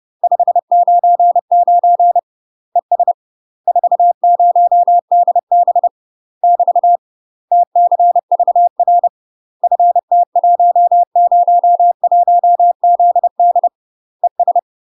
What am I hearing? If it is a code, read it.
Morse code, heard as a keyed tone, about 30 wpm, 700 Hz. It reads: 599 ES 40DB <BT> TCVR FT101ZD ES
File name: UK MASTER QSOs TRAINING / 30WPM / UK MASTER QSOs 0000